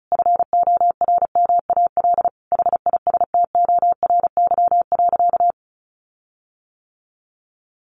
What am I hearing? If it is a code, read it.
Morse code, heard as a keyed tone, about 34 wpm, 700 Hz. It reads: FORMAL HISTORY.
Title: Sentences from Top 1300 Words 35wpm